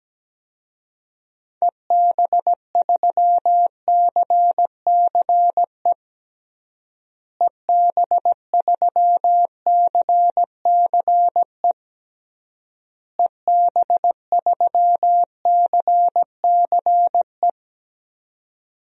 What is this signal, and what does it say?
Morse code, heard as a keyed tone, about 17 wpm, 700 Hz. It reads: EB3CCE EB3CCE EB3CCE
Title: DXCC-100 Call Signs - 6 Characters - Repeated 3x - 17wpm